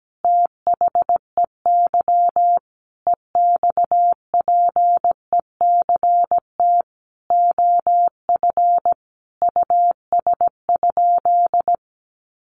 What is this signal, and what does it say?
Morse code, heard as a keyed tone, about 17 wpm, 700 Hz. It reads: THEY EXPECT OF US?
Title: Sentences from Top 1400 Words 17wpm